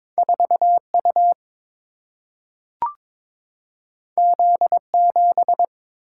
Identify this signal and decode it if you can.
Morse code, heard as a keyed tone, about 22 wpm, 700 Hz. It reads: 4U Z7
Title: Single-Letter Single-Number 22wpm